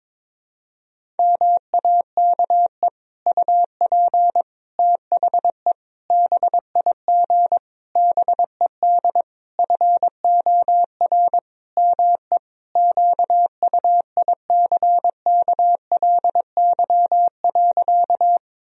Morse code, heard as a keyed tone, about 21 wpm, 700 Hz. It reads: MAKE UP THE BIG BED FOR ME QUICKLY.
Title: Sentences from Top 1000 Words - Review Part 1 - 22wpm